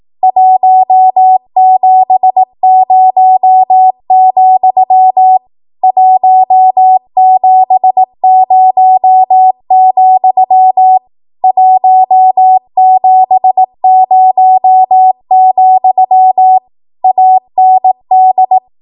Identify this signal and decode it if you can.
Morse code, heard as a keyed tone, about 18 wpm, 750 Hz. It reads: 170, 170, 170, AND
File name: Morse / 18WPM